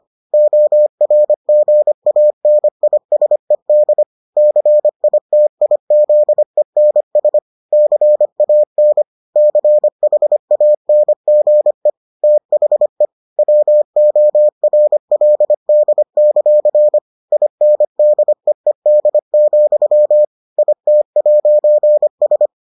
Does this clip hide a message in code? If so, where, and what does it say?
Yes: Morse code from 0.3 to 22.5 s, heard as a keyed tone, about 25 wpm, 600 Hz. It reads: ORGANISED CITIZENS CAN CHANGE THE WORLD; INDEED, IT'S